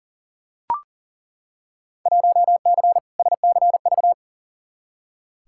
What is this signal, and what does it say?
Morse code, heard as a keyed tone, about 40 wpm, 700 Hz. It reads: 1C SCV
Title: Field Day Exchanges - 40wpm